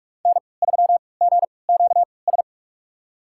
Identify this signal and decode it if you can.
Morse code, heard as a keyed tone, about 46 wpm, spaced as for 25 wpm, 700 Hz. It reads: N3GQS
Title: USA Call Signs - 5 Characters - 3x Character Spacing - 45wpm